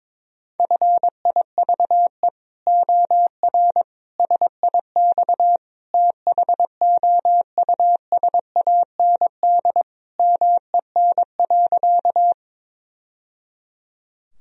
Morse code, heard as a keyed tone, about 22 wpm, 700 Hz. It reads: FIVE OR SIX THOUSAND MEN.
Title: Sentences from Top 1000 Words - Review Part 3 - 22wpm